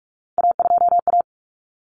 Morse code, heard as a keyed tone, about 44 wpm, 700 Hz. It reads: A2A
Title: Common Cut Numbers - 3-Digit 45wpm